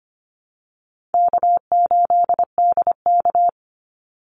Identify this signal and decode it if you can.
Morse code, heard as a keyed tone, about 25 wpm, 700 Hz. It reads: K8DK
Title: Call Signs Repeated 2x - 25wpm